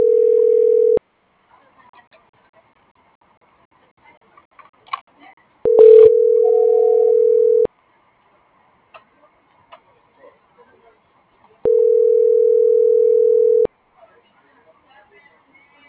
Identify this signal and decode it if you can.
Ringback tone (North America), 440 + 480 Hz.
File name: CENTRAL GENERAL / out daniel.valazco